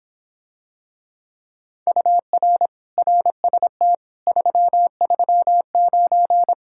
Morse code, heard as a keyed tone, about 26 wpm, 700 Hz. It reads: UR RST 339